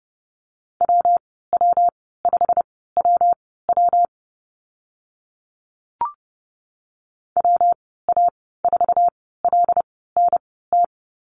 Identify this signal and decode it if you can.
Morse code, heard as a keyed tone, about 30 wpm, spaced as for 17 wpm, 700 Hz. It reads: WW5WW WA4LNT
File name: Lesson 19g - Letter w - Call Signs 30wpm